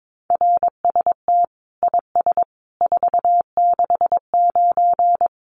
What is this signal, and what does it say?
Morse code, heard as a keyed tone, about 22 wpm, 700 Hz. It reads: RST IS 469
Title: Common QSO Phrases 22wpm